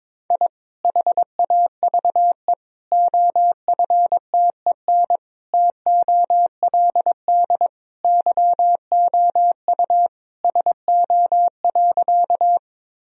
Morse code, heard as a keyed tone, about 21 wpm, 700 Hz. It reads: I HAVE OFTEN TOLD YOU SO.
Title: Sentences from Top 1000 Words - Review Part 1 - 22wpm